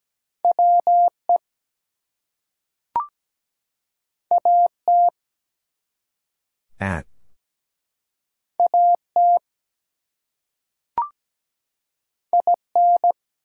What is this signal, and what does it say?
Morse code, heard as a keyed tone, about 17 wpm, 700 Hz. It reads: WE AT AT IN
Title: Two Letter Words - 17wpm